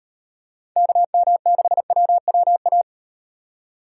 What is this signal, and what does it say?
Morse code, heard as a keyed tone, about 38 wpm, 700 Hz. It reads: KM6WWA